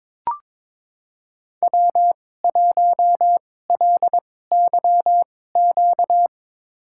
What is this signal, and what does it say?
Morse code, heard as a keyed tone, about 22 wpm, spaced as for 16 wpm, 700 Hz. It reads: W1LYQ
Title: USA Call Signs - 5 Characters - 2x Character Spacing - 22wpm